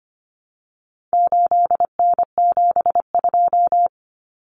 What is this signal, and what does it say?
Morse code, heard as a keyed tone, about 25 wpm, 700 Hz. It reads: 8N72
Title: Common Cut Numbers - 4-Digit 25wpm